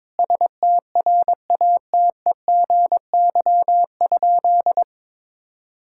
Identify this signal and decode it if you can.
Morse code, heard as a keyed tone, about 21 wpm, 700 Hz. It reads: STRATEGY?
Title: Sentences from Top 1000 Words - Review Part 5 - 22wpm